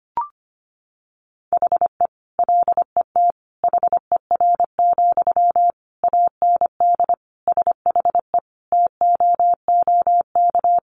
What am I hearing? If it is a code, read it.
Morse code, heard as a keyed tone, about 25 wpm, 700 Hz. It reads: HE LET HER, AND SHE TOOK
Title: Sentences from Top 1500 Words 25wpm